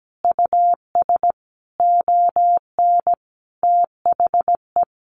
Morse code, heard as a keyed tone, about 17 wpm, 700 Hz. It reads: US ON THE
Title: Sentences from Top 1200 Words 17wpm